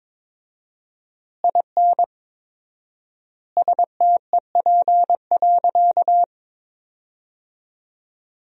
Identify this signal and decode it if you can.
Morse code, heard as a keyed tone, about 22 wpm, 690 Hz. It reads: IN STEP.
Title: 4x Word-Spacing - Sentences from Top 400 Words - 22wpm